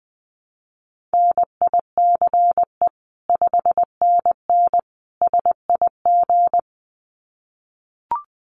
Morse code, heard as a keyed tone, about 20 wpm, 700 Hz. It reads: NICE 5NN SIG
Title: Common QSO Phrases 20wpm